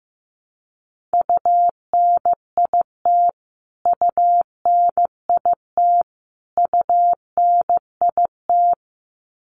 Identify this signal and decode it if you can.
Morse code, heard as a keyed tone, about 15 wpm, 700 Hz. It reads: UNIT UNIT UNIT
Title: Top 500 Words Repeated 15wpm